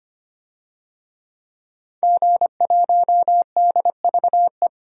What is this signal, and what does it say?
Morse code, heard as a keyed tone, about 25 wpm, 700 Hz. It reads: G1DVE
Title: DXCC-100 Call Signs - 5 Characters - 25wpm